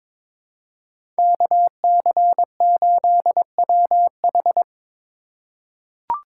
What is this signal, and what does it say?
Morse code, heard as a keyed tone, about 21 wpm, 700 Hz. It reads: KC8WH
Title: K1USN SST QSO Practice - 22wpm